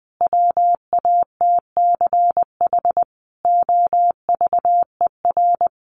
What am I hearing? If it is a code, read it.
Morse code, heard as a keyed tone, about 20 wpm, 700 Hz. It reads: WATCH OVER
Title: Sentences from Top 1500 Words 20wpm